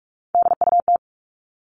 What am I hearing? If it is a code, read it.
Morse code, heard as a keyed tone, about 44 wpm, 700 Hz. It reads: DUT
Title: All Cut Numbers - 3-Digit 45wpm